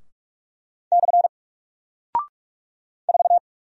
Morse code, heard as a keyed tone, about 44 wpm, 700 Hz. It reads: / 4
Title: Single Character (Letter:Number:Punctuation:Prosign) 45wpm